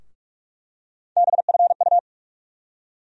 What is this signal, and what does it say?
Morse code, heard as a keyed tone, about 44 wpm, 700 Hz. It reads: BFU
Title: USA Call Sign Postfixes - 3 Letters 45wpm